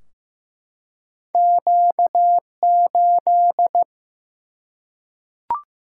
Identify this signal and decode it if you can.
Morse code, heard as a keyed tone, about 15 wpm, 700 Hz. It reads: Q8
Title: Single-Letter Single-Number 15wpm